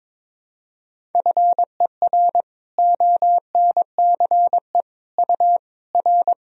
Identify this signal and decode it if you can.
Morse code, heard as a keyed tone, about 22 wpm, 700 Hz. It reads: FER ONCE U R
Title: Sentences from Top 300 Words and Abbreviations 22wpm